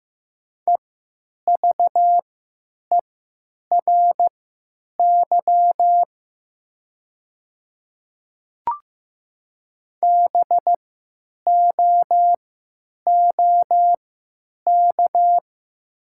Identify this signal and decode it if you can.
Morse code, heard as a keyed tone, about 15 wpm, spaced as for 9 wpm, 700 Hz. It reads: EVERY BOOK
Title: 3x Character Spacing - Top 300 Words - 15wpm